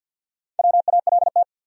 Morse code, heard as a keyed tone, about 47 wpm, 700 Hz. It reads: WALT
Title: USA Common Names - Max 4 Letters - 50wpm